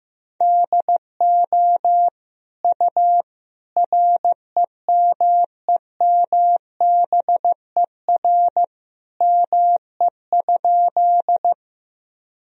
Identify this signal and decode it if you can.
Morse code, heard as a keyed tone, about 15 wpm, 700 Hz. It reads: DO U REMEMBER ME?
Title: Sentences from Top 400 Words and Abbreviations 15wpm